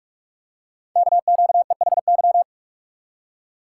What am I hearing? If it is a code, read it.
Morse code, heard as a keyed tone, about 44 wpm, 700 Hz. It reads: KQEHY
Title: Code Groups - 5 Letters - 45wpm